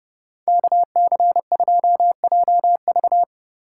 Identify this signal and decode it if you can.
Morse code, heard as a keyed tone, about 29 wpm, 700 Hz. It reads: KC2JV